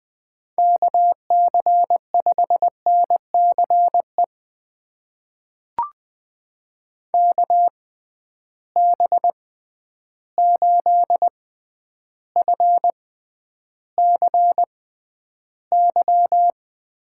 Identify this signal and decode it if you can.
Morse code, heard as a keyed tone, about 20 wpm, spaced as for 10 wpm, 700 Hz. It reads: KC5NCE KB8FCY
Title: USA Call Signs - 6 Characters - 6x Character Spacing - 20wpm